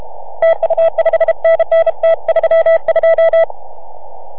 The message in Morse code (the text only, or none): TU 5NN T32